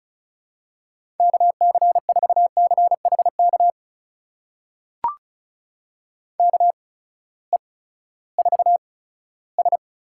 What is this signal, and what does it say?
Morse code, heard as a keyed tone, about 36 wpm, spaced as for 18 wpm, 700 Hz. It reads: KC4CHK KE4S
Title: USA Call Signs - 6 Characters - 8x Character Spacing - 35wpm